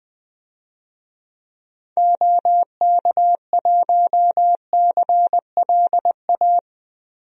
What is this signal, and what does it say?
Morse code, heard as a keyed tone, about 20 wpm, 700 Hz. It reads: OK1CLA
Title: DXCC-100 Call Signs - 6 Characters - Repeated 3x - 20wpm